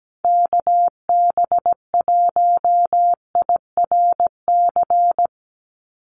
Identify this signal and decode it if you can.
Morse code, heard as a keyed tone, about 17 wpm, 700 Hz. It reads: KB1IRC
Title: USA Call Signs 17wpm